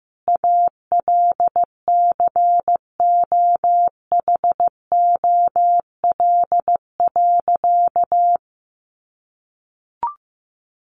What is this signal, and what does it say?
Morse code, heard as a keyed tone, about 15 wpm, 700 Hz. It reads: ALCOHOL.
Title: Sentences from Top 1400 Words 15wpm